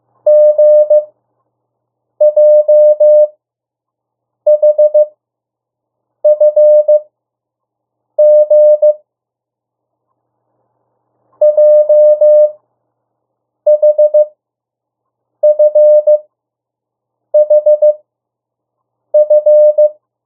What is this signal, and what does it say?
Morse code, heard as a keyed tone, about 15 wpm, spaced as for 6 wpm, 600 Hz. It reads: GJHFG JHFHF
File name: urok 1